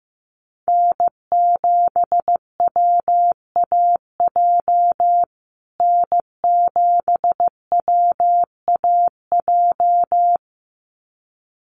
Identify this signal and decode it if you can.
Morse code, heard as a keyed tone, about 15 wpm, 700 Hz. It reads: N7WAJ N7WAJ
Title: USA Call Signs Repeated 2x 15wpm